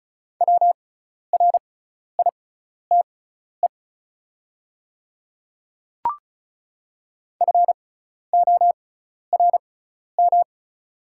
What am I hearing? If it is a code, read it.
Morse code, heard as a keyed tone, about 36 wpm, spaced as for 12 wpm, 700 Hz. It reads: WRITE FORM
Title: 6x Character Spacing - Top 200 Words - 35wpm